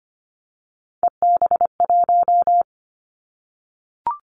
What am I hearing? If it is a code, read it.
Morse code, heard as a keyed tone, about 25 wpm, 700 Hz. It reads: EB1
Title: Callsign Prefixes for Top 25 DXCC - 25wpm